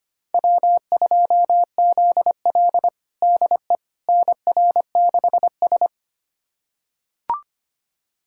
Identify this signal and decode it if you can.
Morse code, heard as a keyed tone, about 25 wpm, 700 Hz. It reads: W2ZL DE NR6S
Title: USA Callsign DE Callsign - 4 Characters - 25wpm